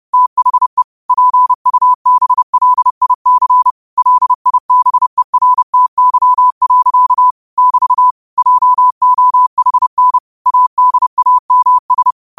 Morse code, heard as a keyed tone, about 30 wpm, 1000 Hz. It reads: THE PUBLIC LIBERTY. <BT> JOHN ADAMS
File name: qotdfiles / 30wpm